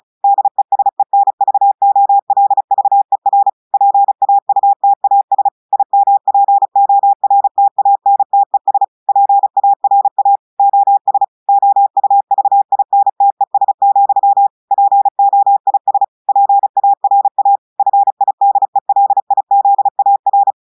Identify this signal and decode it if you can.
Morse code, heard as a keyed tone, about 35 wpm, 800 Hz. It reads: DESENVOLVER PAUTAS IMPORTANTES PARA OS OUVINTES, POIS PARA FIDELIZAR
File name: audios / portugais / portugais texte en clair 2 vitesse 35 mots minutes